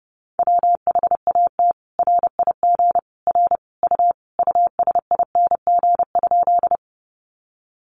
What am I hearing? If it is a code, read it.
Morse code, heard as a keyed tone, about 30 wpm, 700 Hz. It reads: WHAT RIG R U USING?
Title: Common QSO Phrases 30wpm